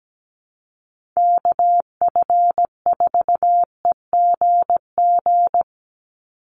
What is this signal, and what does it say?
Morse code, heard as a keyed tone, about 17 wpm, 700 Hz. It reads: KF4EGG
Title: USA Call Signs 17wpm